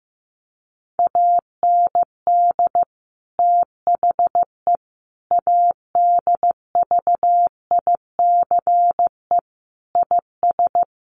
Morse code, heard as a keyed tone, about 15 wpm, 700 Hz. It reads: AND THE ADVICE IS
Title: Sentences from Top 1100 Words 15wpm